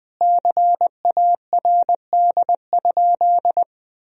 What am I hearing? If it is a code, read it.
Morse code, heard as a keyed tone, about 21 wpm, 700 Hz. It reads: CARD?
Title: Sentences from Top 1000 Words - Review Part 4 - 20wpm